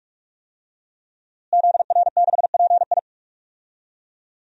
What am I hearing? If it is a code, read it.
Morse code, heard as a keyed tone, about 42 wpm, 700 Hz. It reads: ZR6PI
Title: DXCC-100 Call Signs - 5 Characters - Repeated 2x - 45wpm